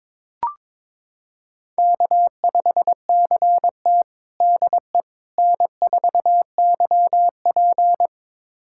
Morse code, heard as a keyed tone, about 22 wpm, 700 Hz. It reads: K5CT DE N4YP
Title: USA Callsign DE Callsign - 4 Characters - 22wpm